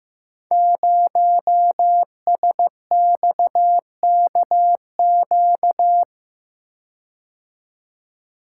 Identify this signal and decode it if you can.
Morse code, heard as a keyed tone, about 15 wpm, 700 Hz. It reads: 0SXKQ
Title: Code Groups - 5 Letter:Numbers - 15wpm